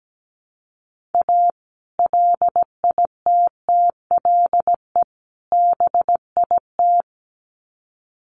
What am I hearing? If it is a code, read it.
Morse code, heard as a keyed tone, about 17 wpm, 700 Hz. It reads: A LITTLE BIT
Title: Sets of 3 Words 17wpm